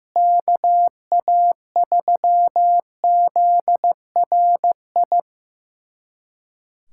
Morse code, heard as a keyed tone, about 15 wpm, 700 Hz. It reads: KA3ZRI
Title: USA Callsign DE Callsign - 6 Characters - 15wpm